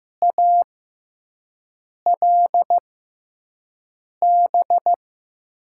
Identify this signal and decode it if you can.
Morse code, heard as a keyed tone, about 15 wpm, spaced as for 5 wpm, 700 Hz. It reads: ALB
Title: USA Call Signs - 6x Character Spacing - 15wpm